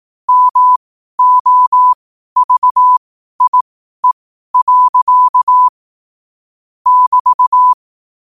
Morse code, heard as a keyed tone, about 18 wpm, spaced as for 12 wpm, 1000 Hz. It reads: MOVIE. <BT>
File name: qotdfiles / 12wpm